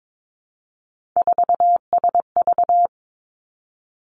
Morse code, heard as a keyed tone, about 22 wpm, 700 Hz. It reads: 4SV
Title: Three Challenging Letter-Number Combinations 22wpm